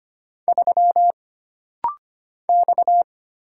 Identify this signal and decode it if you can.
Morse code, heard as a keyed tone, about 25 wpm, 700 Hz. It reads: 3 X
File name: Single Character (Letter:Number:Punctuation:Prosign) - Rapid-Fire - 25wpm